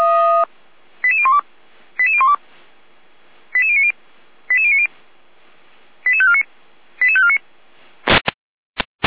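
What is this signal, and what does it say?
Five-tone selective call (ZVEI), heard as quick runs of five tones, tones of about 70 ms. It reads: 88012 88012 88090 88090 88549 88549
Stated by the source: ZVEI five-tone selective call 88335 received by SDR, archived by Feuerwehr Orlamünde.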